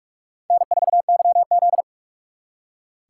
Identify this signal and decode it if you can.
Morse code, heard as a keyed tone, about 56 wpm, 700 Hz. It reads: N4YZ